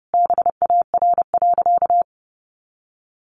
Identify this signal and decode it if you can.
Morse code, heard as a keyed tone, about 30 wpm, 700 Hz. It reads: BAR.
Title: Additional Easy Phrases 30wpm